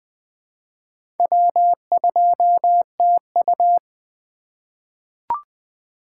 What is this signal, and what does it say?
Morse code, heard as a keyed tone, about 21 wpm, 700 Hz. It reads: W2TU
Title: USA Call Signs - 4 Characters - 3x Character Spacing - 20wpm